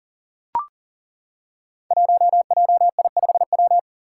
Morse code, heard as a keyed tone, about 40 wpm, 700 Hz. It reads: 1JI5W
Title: Five Challenging Letter-Number Combinations 40wpm